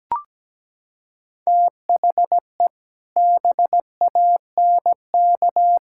Morse code, heard as a keyed tone, about 17 wpm, 700 Hz. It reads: THE BANK
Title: Sentences from Top 1000 Words - Review Part 6 - 17wpm